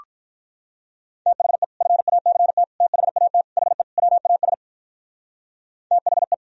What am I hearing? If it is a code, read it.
Morse code, heard as a keyed tone, about 47 wpm, 700 Hz. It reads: THE FACT THAT HE WAS THE